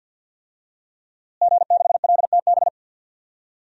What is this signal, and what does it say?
Morse code, heard as a keyed tone, about 51 wpm, 700 Hz. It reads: G6LTB
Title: DXCC-100 Call Signs - 5 Characters - Repeated 2x - 50wpm